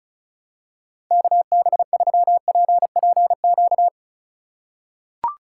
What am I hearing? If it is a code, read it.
Morse code, heard as a keyed tone, about 34 wpm, 700 Hz. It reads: KB3PPQ
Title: USA Call Signs - 6 Characters - 4x Character Spacing - 35wpm